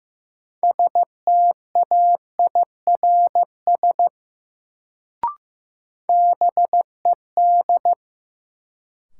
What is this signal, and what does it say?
Morse code, heard as a keyed tone, about 15 wpm, 700 Hz. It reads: STAIRS BED
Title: Sets of 5 Words Encore - New Word Drill - 15wpm